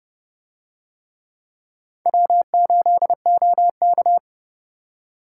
Morse code, heard as a keyed tone, about 31 wpm, 700 Hz. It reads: W8OK